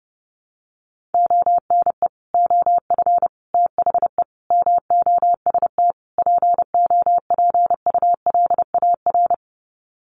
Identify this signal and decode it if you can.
Morse code, heard as a keyed tone, about 30 wpm, 700 Hz. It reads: ONE OF THE MOST POPULAR
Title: Sets of 5 Words 30wpm